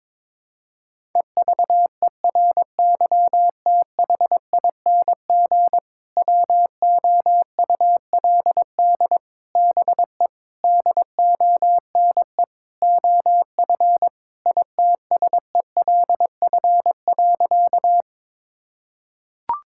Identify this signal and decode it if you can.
Morse code, heard as a keyed tone, about 22 wpm, 700 Hz. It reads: EVERYTHING WOULD BE DONE OF ITSELF.
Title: Sentences from Top 1000 Words - Review Part 5 - 22wpm